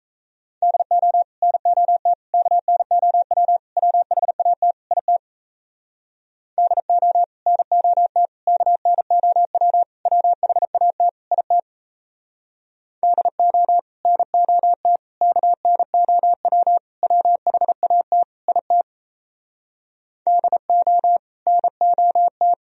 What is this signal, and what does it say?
Morse code, heard as a keyed tone, about 36 wpm, 700 Hz. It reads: DO NOT KNOW WHAT IT DO NOT KNOW WHAT IT DO NOT KNOW WHAT IT DO NOT